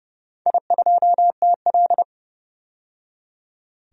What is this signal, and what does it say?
Morse code, heard as a keyed tone, about 30 wpm, 700 Hz. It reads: I2TL
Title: DXCC-100 Call Signs - 4 Characters - 30wpm